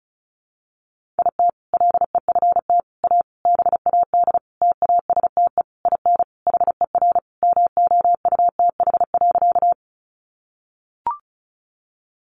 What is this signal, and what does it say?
Morse code, heard as a keyed tone, about 34 wpm, 700 Hz. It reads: IT LEFT A BAD TASTE IN HER MOUTH.
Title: Sentences from Top 1500 Words 35wpm